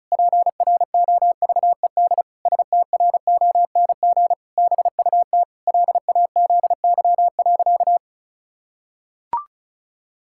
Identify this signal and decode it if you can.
Morse code, heard as a keyed tone, about 36 wpm, 700 Hz. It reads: PROVED STRONG BUT LAZY.
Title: Pangram Challenge v2 - 35wpm